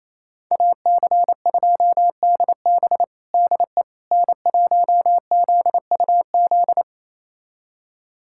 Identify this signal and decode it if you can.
Morse code, heard as a keyed tone, about 28 wpm, 700 Hz. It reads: AC2DB DE N1ZUZ
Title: USA Callsign DE Callsign - 5 Characters - 28wpm